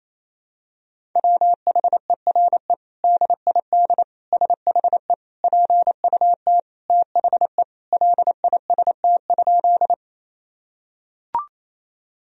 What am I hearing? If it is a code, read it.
Morse code, heard as a keyed tone, about 28 wpm, 700 Hz. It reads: WHERE DID SHE PUT THE LIST?
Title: Sentences from Top 1000 Words - Review Part 2 - 28wpm